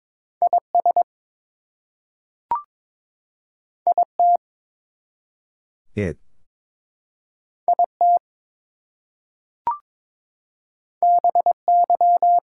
Morse code, heard as a keyed tone, about 22 wpm, 700 Hz. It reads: IS IT IT BY
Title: Two Letter Words - 22wpm